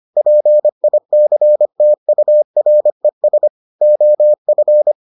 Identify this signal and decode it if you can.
Morse code, heard as a keyed tone, about 25 wpm, 600 Hz. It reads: PICTURES OF